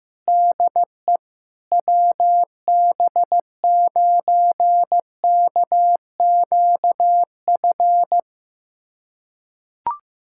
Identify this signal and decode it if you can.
Morse code, heard as a keyed tone, about 15 wpm, 700 Hz. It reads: DE WB9KQF
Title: USA Callsign DE Callsign - 6 Characters - 15wpm